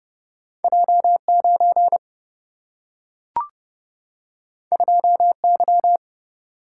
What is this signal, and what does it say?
Morse code, heard as a keyed tone, about 30 wpm, 700 Hz. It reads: J 9 2 Y
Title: Single-Letter Single-Number 30wpm